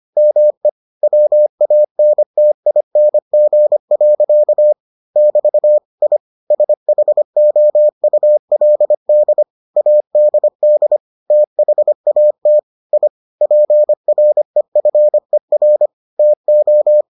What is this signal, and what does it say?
Morse code, heard as a keyed tone, about 25 wpm, 600 Hz. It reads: ME WANTING. <BT> I SHOULD ADD THAT I PREFER TO